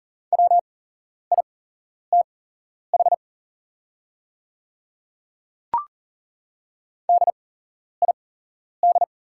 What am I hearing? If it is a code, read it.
Morse code, heard as a keyed tone, about 41 wpm, spaced as for 11 wpm, 700 Hz. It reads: WITH DID
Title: 8x Character Spacing - Top 100 Words - 40wpm